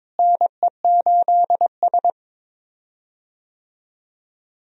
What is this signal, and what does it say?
Morse code, heard as a keyed tone, about 21 wpm, 700 Hz. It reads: NE8S
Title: USA Call Signs - 4 Characters - Repeated 3x - 22wpm